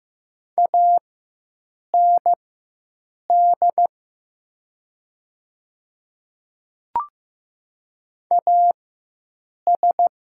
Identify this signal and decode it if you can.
Morse code, heard as a keyed tone, about 15 wpm, spaced as for 7 wpm, 700 Hz. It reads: AND AS
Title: 4x Character Spacing - Top 100 Words - 15wpm